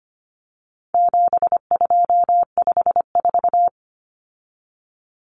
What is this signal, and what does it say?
Morse code, heard as a keyed tone, about 25 wpm, 700 Hz. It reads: 7254
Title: Common Cut Numbers - 4-Digit 25wpm